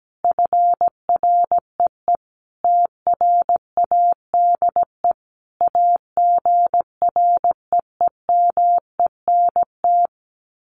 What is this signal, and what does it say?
Morse code, heard as a keyed tone, about 17 wpm, 700 Hz. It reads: FREE TRADE AGREEMENT
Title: Sets of 5 Words Encore 17wpm